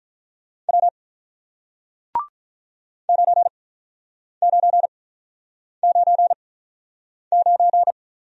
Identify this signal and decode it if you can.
Morse code, heard as a keyed tone, about 48 wpm, spaced as for 9 wpm, 700 Hz. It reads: W 9999